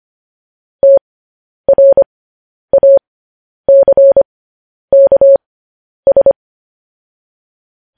Morse code, heard as a keyed tone, about 25 wpm, spaced as for 10 wpm, 550 Hz. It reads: TRACKS
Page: news headlines in morse code at 10 wpm